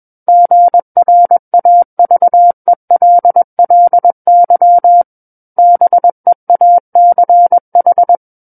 Morse code, heard as a keyed tone, about 21 wpm, 700 Hz. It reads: GRAVELLY BEACH
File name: Book One / 12-Chapter 12 - 21 WPM